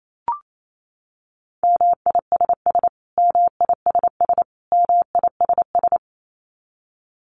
Morse code, heard as a keyed tone, about 28 wpm, 700 Hz. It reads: MISS MISS MISS
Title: Top 500 Words Repeated 28wpm